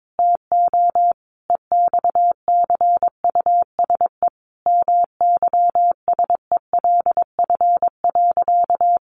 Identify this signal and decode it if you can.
Morse code, heard as a keyed tone, about 22 wpm, 700 Hz. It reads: TO EXCUSE MYSELF.
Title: Sentences from Top 1300 Words 22wpm